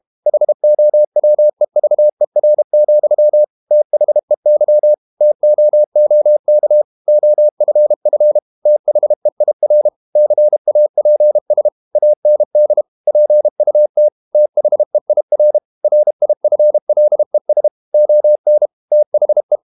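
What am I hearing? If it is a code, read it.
Morse code, heard as a keyed tone, about 32 wpm, 600 Hz. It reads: HOWEVER, THEY TOOK OFF THEIR CAPS AND PUT THEIR RIFLES ON THE
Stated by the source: Morse code practice at 32 wpm from CW med Gnister, reading Ken Follett's novel Never in English.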